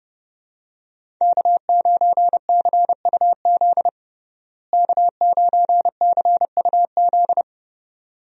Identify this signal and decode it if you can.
Morse code, heard as a keyed tone, about 31 wpm, 700 Hz. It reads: K9CUZ K9CUZ
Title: USA Call Signs - 5 Characters - Repeated 2x - 30wpm